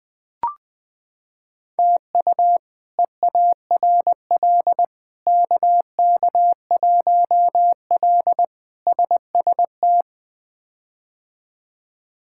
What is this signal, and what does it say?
Morse code, heard as a keyed tone, about 20 wpm, 700 Hz. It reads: TU EARL KK1L SST
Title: K1USN SST QSO Practice - 20wpm